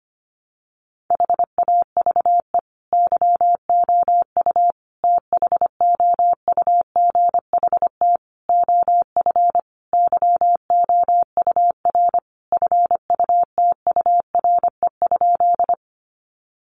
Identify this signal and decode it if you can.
Morse code, heard as a keyed tone, about 25 wpm, 700 Hz. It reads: HAVE YOU THOUGHT OF YOUR FUTURE?
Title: Sentences from Top 700 Words 25wpm